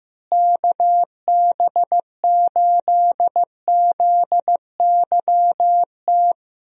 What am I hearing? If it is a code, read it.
Morse code, heard as a keyed tone, about 16 wpm, 700 Hz. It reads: KB8ZYT
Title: USA Call Signs - 6 Characters - Repeated 3x - 15wpm